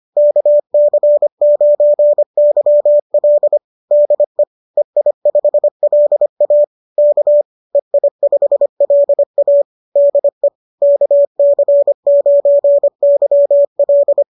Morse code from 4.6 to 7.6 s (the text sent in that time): EI5LA K